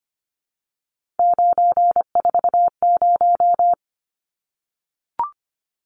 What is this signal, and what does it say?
Morse code, heard as a keyed tone, about 25 wpm, 700 Hz. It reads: 940
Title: Numbers - 3-Digit 25wpm